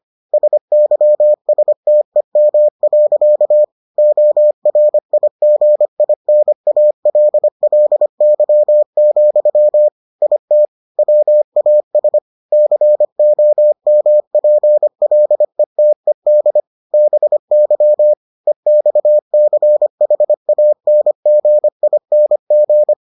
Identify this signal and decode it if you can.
Morse code, heard as a keyed tone, about 25 wpm, 600 Hz. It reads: SYSTEM. ORIGINALLY, IT WAS COMPLETED BY EXCHANGING